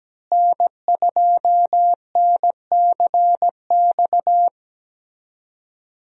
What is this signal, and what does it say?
Morse code, heard as a keyed tone, about 17 wpm, 700 Hz. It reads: N2NCX
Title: USA Call Signs - 5 Characters - Repeated 2x - 17wpm